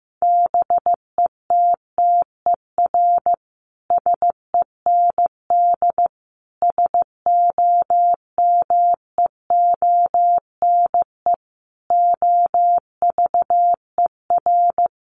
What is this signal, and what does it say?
Morse code, heard as a keyed tone, about 15 wpm, 700 Hz. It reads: BETTER SEND SOMEONE OVER
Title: Sentences from Top 1500 Words 15wpm